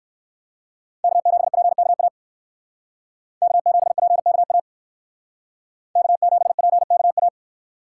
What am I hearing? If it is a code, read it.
Morse code, heard as a keyed tone, about 57 wpm, 700 Hz. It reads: K7PKA K7PKA K7PKA